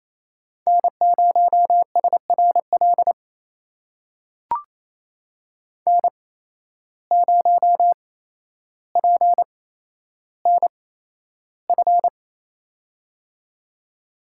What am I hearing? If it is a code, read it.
Morse code, heard as a keyed tone, about 27 wpm, spaced as for 12 wpm, 700 Hz. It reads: N0SRL N0PNF
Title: USA Call Signs - 5 Characters - 8x Character Spacing - 28wpm